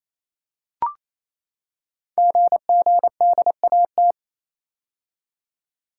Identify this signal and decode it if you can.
Morse code, heard as a keyed tone, about 27 wpm, 700 Hz. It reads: GGDAT
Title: Code Groups - 5 Letters - 28wpm